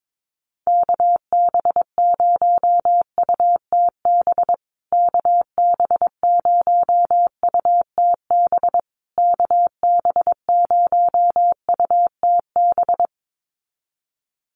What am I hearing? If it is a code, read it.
Morse code, heard as a keyed tone, about 22 wpm, 700 Hz. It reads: KB0UTB KB0UTB KB0UTB
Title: USA Call Signs Repeated 3x 22wpm